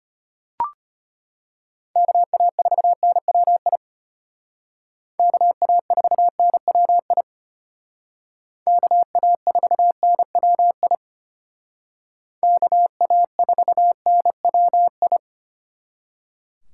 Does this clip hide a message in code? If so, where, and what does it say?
Yes: Morse code from 2.0 to 15.2 s, heard as a keyed tone, about 33 wpm, 700 Hz. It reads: KA4NWI KA4NWI KA4NWI KA4NWI